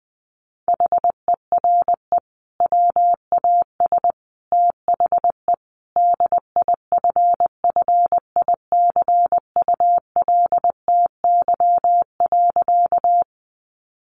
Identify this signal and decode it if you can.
Morse code, heard as a keyed tone, about 20 wpm, 700 Hz. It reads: HERE WAS THE DIFFICULTY.
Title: Sentences from Top 1500 Words 20wpm